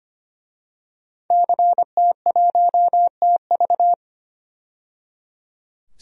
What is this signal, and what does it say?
Morse code, heard as a keyed tone, about 25 wpm, 700 Hz. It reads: CT1TV
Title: DXCC-100 Call Signs - 5 Characters - 25wpm